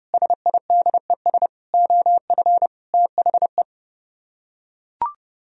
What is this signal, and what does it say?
Morse code, heard as a keyed tone, about 31 wpm, 700 Hz. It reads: SIDES OF THE